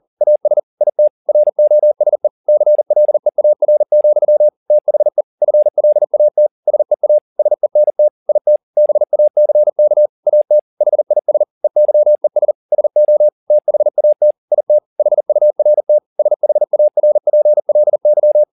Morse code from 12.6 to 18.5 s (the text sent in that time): SO THAT IT HURT SHARPLY